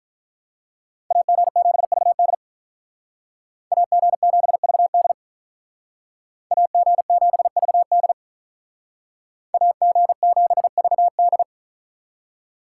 Morse code, heard as a keyed tone, about 44 wpm, 700 Hz. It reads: AG7VD AG7VD AG7VD AG7VD